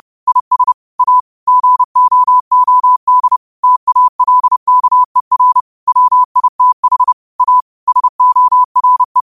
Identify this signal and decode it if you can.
Morse code, heard as a keyed tone, about 30 wpm, 1000 Hz. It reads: IS A GOOD TALKER WITH A SORE
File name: qotdfiles / 30wpm